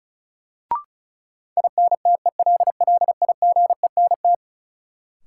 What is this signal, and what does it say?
Morse code, heard as a keyed tone, about 36 wpm, 700 Hz. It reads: INTELLIGENT
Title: Sentences from Top 1000 Words - Review Part 7 - New Word Drill - 35wpm